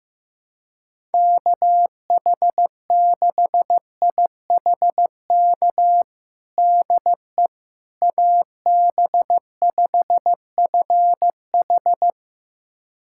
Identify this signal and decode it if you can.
Morse code, heard as a keyed tone, about 15 wpm, 700 Hz. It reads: KH6IHK DE AB5FH
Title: USA Callsign DE Callsign 15wpm